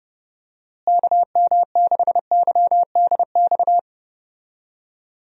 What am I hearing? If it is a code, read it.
Morse code, heard as a keyed tone, about 32 wpm, 700 Hz. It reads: KM6YDX
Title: USA Call Signs - 6 Characters - Repeated 3x - 30wpm